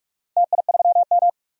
Morse code, heard as a keyed tone, about 44 wpm, 700 Hz. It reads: TI3M
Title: DXCC-100 Call Signs - 4 Characters - 45wpm